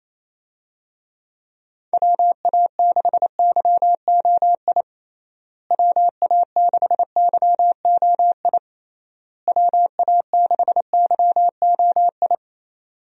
Morse code, heard as a keyed tone, about 28 wpm, 700 Hz. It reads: WA6YOI WA6YOI WA6YOI
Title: USA Call Signs - 6 Characters - Repeated 3x - 28wpm